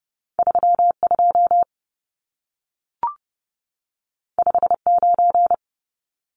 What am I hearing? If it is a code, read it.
Morse code, heard as a keyed tone, about 30 wpm, 700 Hz. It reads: 32 59
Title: Numbers - 2-Digit 30wpm